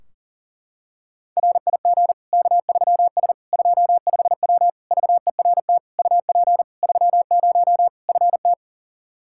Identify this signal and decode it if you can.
Morse code, heard as a keyed tone, about 37 wpm, 700 Hz. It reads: RIG K3S 25W VERT UP 30 FT
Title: Practice QSO - Exchange 2 - Info - 40wpm